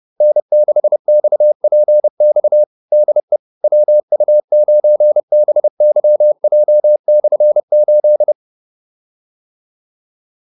Morse code, heard as a keyed tone, about 30 wpm, 600 Hz. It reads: N6XPX DE WU9BYJ/8